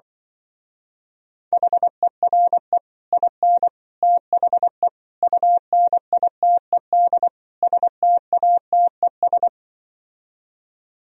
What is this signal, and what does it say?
Morse code, heard as a keyed tone, about 24 wpm, 700 Hz. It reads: HERE IN THE UNITED STATES
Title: Speed-Racing - Sets of 5 Words 20wpm